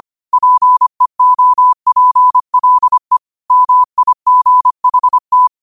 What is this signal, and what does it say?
Morse code, heard as a keyed tone, about 25 wpm, 1000 Hz. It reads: PEOPLE MIGHT
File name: qotdfiles / 25wpm